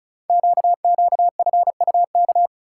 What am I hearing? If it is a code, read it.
Morse code, heard as a keyed tone, about 36 wpm, 700 Hz. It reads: QQFUK
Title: Code Groups - 5 Letters - 35wpm